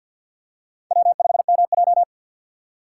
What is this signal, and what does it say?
Morse code, heard as a keyed tone, about 50 wpm, 700 Hz. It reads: W5MJ
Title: USA Call Signs - 4 Characters - 3x Character Spacing - 50wpm